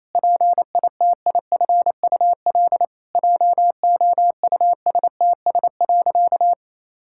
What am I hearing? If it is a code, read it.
Morse code, heard as a keyed tone, about 28 wpm, 700 Hz. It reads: PITIFUL JOUSTS.
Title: Pangram Challenge v2 - 28wpm